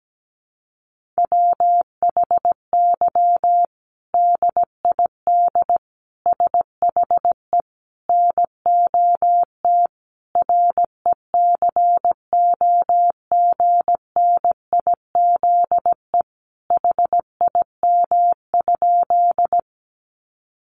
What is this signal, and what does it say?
Morse code, heard as a keyed tone, about 17 wpm, 700 Hz. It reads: WHY DID SHE NOT RECOGNIZE HIM?
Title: Sentences from Top 1400 Words 17wpm